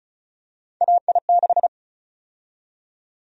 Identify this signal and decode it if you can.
Morse code, heard as a keyed tone, about 34 wpm, 700 Hz. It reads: AI6
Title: USA Call Sign Prefixes - Repeated 2x - 35wpm